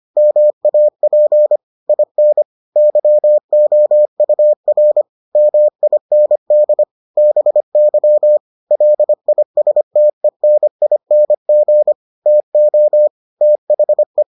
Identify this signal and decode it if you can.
Morse code, heard as a keyed tone, about 25 wpm, 600 Hz. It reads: MAP IN YOUR MIND BY LISTENING TO THE